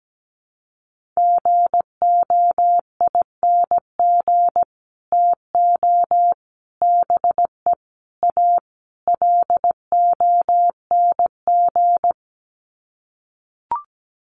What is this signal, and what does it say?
Morse code, heard as a keyed tone, about 17 wpm, 700 Hz. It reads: GOING TO BE A LONG
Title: Sets of 5 Words Encore2 17wpm